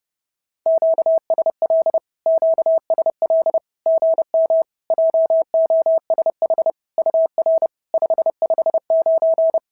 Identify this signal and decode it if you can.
Morse code, heard as a keyed tone, about 30 wpm, 650 Hz. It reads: QSL QSL GM JOSH UR 559
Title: SOTA QSO Practice 30wpm